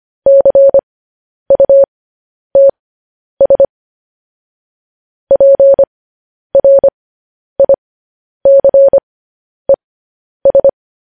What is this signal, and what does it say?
Morse code, heard as a keyed tone, about 25 wpm, spaced as for 10 wpm, 550 Hz. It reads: CUTS PRICES
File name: podcast / News headlines-1x-25x10